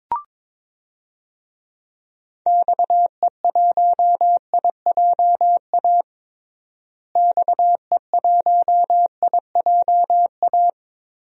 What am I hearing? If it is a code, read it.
Morse code, heard as a keyed tone, about 22 wpm, 700 Hz. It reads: XE1IJA XE1IJA
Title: DXCC-100 Call Signs - 6 Characters - Repeated 2x - 22wpm